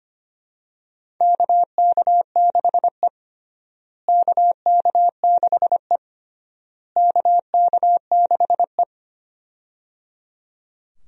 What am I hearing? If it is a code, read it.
Morse code, heard as a keyed tone, about 25 wpm, 700 Hz. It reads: KK6E KK6E KK6E
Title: USA Call Signs - 4 Characters - Repeated 3x - 25wpm